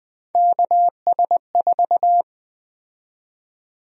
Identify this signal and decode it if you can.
Morse code, heard as a keyed tone, about 20 wpm, 700 Hz. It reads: KS4
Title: USA Call Sign Prefixes - 20wpm